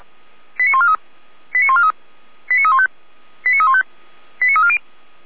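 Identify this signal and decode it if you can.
Five-tone selective call (ZVEI), heard as quick runs of five tones, tones of about 80 ms. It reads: 89143 89143 89315 89315 89340
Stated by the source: ZVEI five-tone selective call received by SDR.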